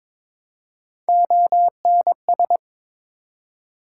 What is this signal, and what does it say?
Morse code, heard as a keyed tone, about 22 wpm, 700 Hz. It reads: ONS
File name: ARRL Sections v2 - 22wpm